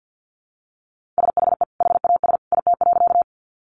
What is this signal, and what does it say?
Morse code, heard as a keyed tone, about 50 wpm, 700 Hz. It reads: SHE HAS IT.
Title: Sentences from Top 100 Words 50wpm